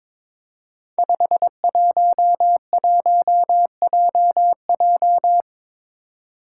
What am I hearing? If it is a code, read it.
Morse code, heard as a keyed tone, about 22 wpm, 700 Hz. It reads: 511JJ
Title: Five Challenging Letter-Number Combinations 22wpm